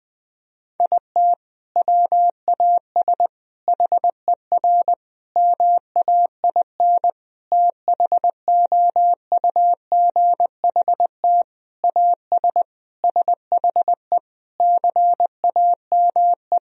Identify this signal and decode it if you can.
Morse code, heard as a keyed tone, about 21 wpm, 700 Hz. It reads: IT WAS HER MAIN THOUGHT AS SHE CAME
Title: Sentences from Top 1000 Words - Review Part 2 - 20wpm